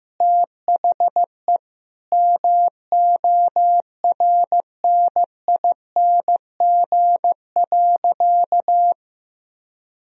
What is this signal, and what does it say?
Morse code, heard as a keyed tone, about 16 wpm, 700 Hz. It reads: THE MORNING.
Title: Sentences from Top 1000 Words - Review Part 2 - 15wpm